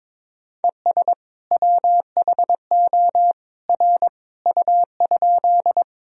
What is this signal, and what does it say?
Morse code, heard as a keyed tone, about 22 wpm, 700 Hz. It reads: ES WHO R U?
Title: Sentences from Top 100 Words and Abbreviations 22wpm